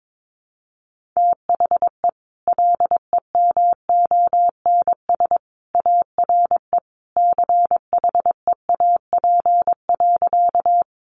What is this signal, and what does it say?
Morse code, heard as a keyed tone, about 22 wpm, 700 Hz. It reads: THE LEMONS ARE CHEAP.
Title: Additional Easy Phrases 22wpm